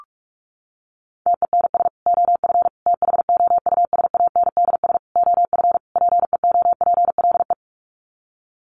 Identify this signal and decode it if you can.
Morse code, heard as a keyed tone, about 44 wpm, 700 Hz. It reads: TENS OF THOUSANDS OF PEOPLE
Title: Sets of 5 Words Encore2 45wpm